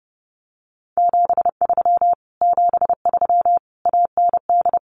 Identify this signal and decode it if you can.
Morse code, heard as a keyed tone, about 30 wpm, 700 Hz. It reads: 73 73 AND
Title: Common QSO Phrases Encore 30wpm